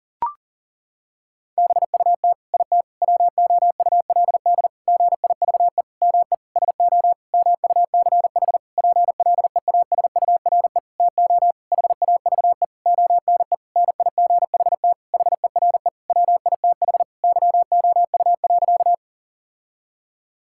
Morse code, heard as a keyed tone, about 40 wpm, 700 Hz. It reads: BUT IT WOULD GIVE ME SO MUCH PLEASURE TO HAVE ONE NIGHT HERE WITH YOU.
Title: Sentences from Top 1600 Words 40wpm